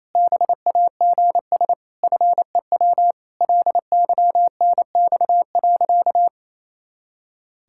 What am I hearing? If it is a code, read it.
Morse code, heard as a keyed tone, about 28 wpm, 700 Hz. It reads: BAGS FEW LYNX.
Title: Pangram Challenge v2 - 28wpm